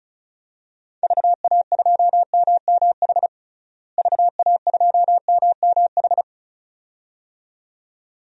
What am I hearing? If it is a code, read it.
Morse code, heard as a keyed tone, about 34 wpm, 700 Hz. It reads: VA2MMH VA2MMH
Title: DXCC-100 Call Signs - 6 Characters - Repeated 2x - 35wpm